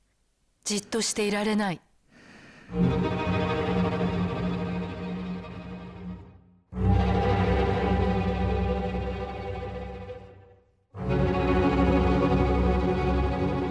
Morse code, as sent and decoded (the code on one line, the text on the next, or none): none